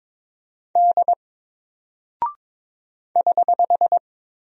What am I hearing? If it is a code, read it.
Morse code, heard as a keyed tone, about 22 wpm, 700 Hz. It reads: D <HH>
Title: Single Character (Letter:Number:Punctuation:Prosign) 22wpm